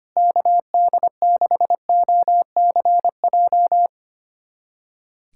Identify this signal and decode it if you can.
Morse code, heard as a keyed tone, about 25 wpm, 700 Hz. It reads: KD6OCJ
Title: USA Callsign DE Callsign - 6 Characters - 25wpm